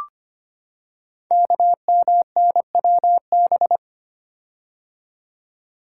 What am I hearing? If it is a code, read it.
Morse code, heard as a keyed tone, about 25 wpm, 700 Hz. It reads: KMNWB
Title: Code Groups - 5 Letters - 25wpm